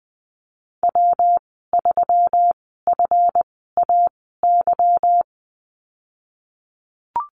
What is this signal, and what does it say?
Morse code, heard as a keyed tone, about 20 wpm, spaced as for 15 wpm, 700 Hz. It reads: W3FAY
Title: 173 Lesson 28g - Number 9 - Call Signs 20wpm